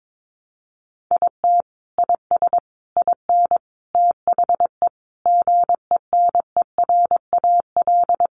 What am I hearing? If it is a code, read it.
Morse code, heard as a keyed tone, about 22 wpm, 700 Hz. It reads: IT IS IN THE GENERAL
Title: Sentences from Top 1200 Words 22wpm